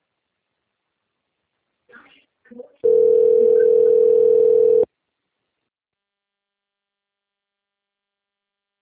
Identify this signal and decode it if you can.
Ringback tone (North America), 440 + 480 Hz, with speech.
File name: out- - - - - .48761